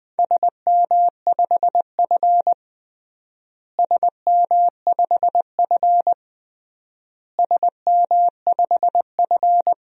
Morse code, heard as a keyed tone, about 20 wpm, 700 Hz. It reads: SM5F SM5F SM5F
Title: DXCC-100 Call Signs - 4 Characters - Repeated 3x - 20wpm